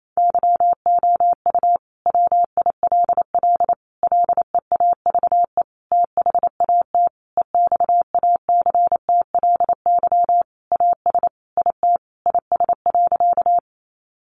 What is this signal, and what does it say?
Morse code, heard as a keyed tone, about 28 wpm, 700 Hz. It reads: YOU WILL LEAVE THAT EXACTLY AS IT IS.
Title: Sentences from Top 900 Words 28wpm